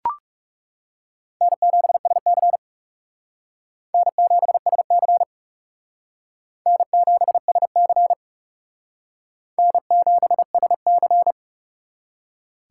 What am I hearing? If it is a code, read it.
Morse code, heard as a keyed tone, about 37 wpm, 700 Hz. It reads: N7SC N7SC N7SC N7SC
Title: Speed Racing - USA 1x2 Call Signs 30wpm